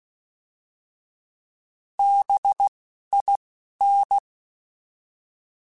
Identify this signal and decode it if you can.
Morse code, heard as a keyed tone, about 16 wpm, spaced as for 11 wpm, 800 Hz. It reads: BIN
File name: Alphabetical / Alphabetical mp3s / Bs @10